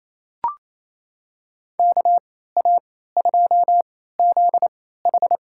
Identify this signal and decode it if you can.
Morse code, heard as a keyed tone, about 28 wpm, spaced as for 16 wpm, 700 Hz. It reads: KA2ZH
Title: USA Call Signs - 3x Character Spacing - 28wpm